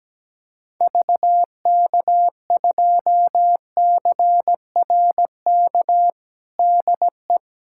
Morse code, heard as a keyed tone, about 17 wpm, 700 Hz. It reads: VK2CRK DE
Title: USA Callsign DE Callsign 17wpm